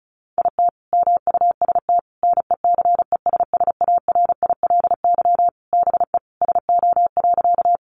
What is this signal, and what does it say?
Morse code, heard as a keyed tone, about 34 wpm, 700 Hz. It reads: IT MUST NECESSARILY BE SO.
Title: Sentences from Top 1400 Words 35wpm